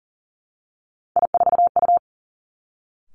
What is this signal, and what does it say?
Morse code, heard as a keyed tone, about 42 wpm, 690 Hz. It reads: I4U
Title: Three Challenging Letter-Number Combinations 40wpm